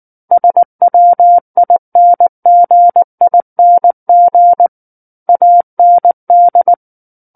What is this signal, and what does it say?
Morse code, heard as a keyed tone, about 19 wpm, 700 Hz. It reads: SWINGING AND
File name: Book One / 10-Chapter 10 - 19 WPM